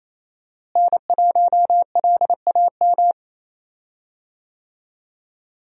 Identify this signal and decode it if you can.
Morse code, heard as a keyed tone, about 28 wpm, 700 Hz. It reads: N1LAM
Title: USA Call Signs - 5 Characters - Repeated 3x - 28wpm